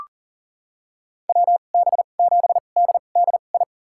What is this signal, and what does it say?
Morse code, heard as a keyed tone, about 37 wpm, spaced as for 28 wpm, 700 Hz. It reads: WB7DDI
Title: USA Call Signs - 6 Characters - 2x Character Spacing - 40wpm